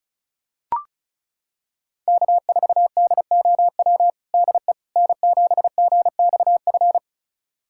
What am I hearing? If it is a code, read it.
Morse code, heard as a keyed tone, about 34 wpm, 700 Hz. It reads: K4DOW DE N7GXF
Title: USA Callsign DE Callsign - 5 Characters - 35wpm